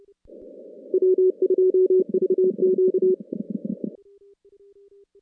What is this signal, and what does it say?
Morse code, heard as a keyed tone, about 28 wpm, 410 Hz. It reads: W2VQ
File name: contests / CW-SS-k5tr- -2200